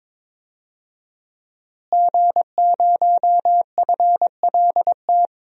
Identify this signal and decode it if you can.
Morse code, heard as a keyed tone, about 22 wpm, 700 Hz. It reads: G0FLT
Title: DXCC-100 Call Signs - 5 Characters - Repeated 2x - 22wpm